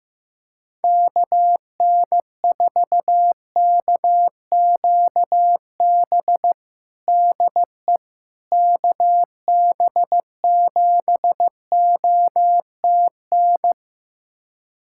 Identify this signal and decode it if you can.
Morse code, heard as a keyed tone, about 15 wpm, 700 Hz. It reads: KN4KQB DE KB7OTN
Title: USA Callsign DE Callsign 15wpm